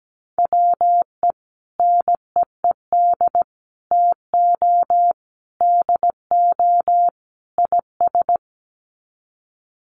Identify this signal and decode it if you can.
Morse code, heard as a keyed tone, about 17 wpm, 700 Hz. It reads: WE NEED TO DO IS
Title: Sets of 5 Words Encore2 17wpm